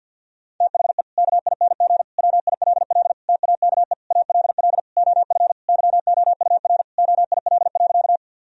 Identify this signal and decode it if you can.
Morse code, heard as a keyed tone, about 50 wpm, 700 Hz. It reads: THE KING WILL TAKE ALL OF YOUR OIL.
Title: Sentences from Top 1000 Words - Review Part 4 - 50wpm